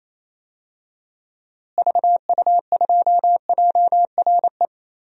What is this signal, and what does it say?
Morse code, heard as a keyed tone, about 28 wpm, 700 Hz. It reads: VU2JRE
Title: DXCC-100 Call Signs - 6 Characters - Repeated 3x - 28wpm